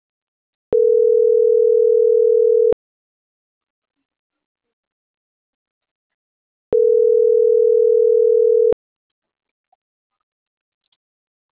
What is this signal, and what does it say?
Ringback tone (North America), 440 + 480 Hz.